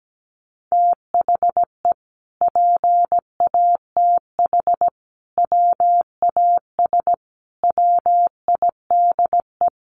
Morse code, heard as a keyed tone, about 17 wpm, 700 Hz. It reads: THE PATH WAS WIDE
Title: Sentences from Top 1400 Words 17wpm